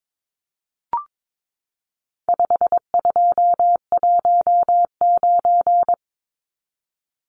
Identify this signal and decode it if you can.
Morse code, heard as a keyed tone, about 22 wpm, 700 Hz. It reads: 5219
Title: Numbers - 4-Digit 22wpm